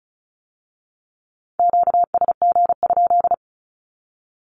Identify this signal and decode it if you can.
Morse code, heard as a keyed tone, about 34 wpm, 700 Hz. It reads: QSG?
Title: Q-Codes with Spoken Meaning - 35wpm